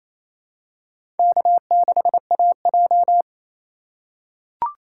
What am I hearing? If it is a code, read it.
Morse code, heard as a keyed tone, about 28 wpm, 700 Hz. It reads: K6AJ
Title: USA Call Signs - 4 Characters - 3x Character Spacing - 28wpm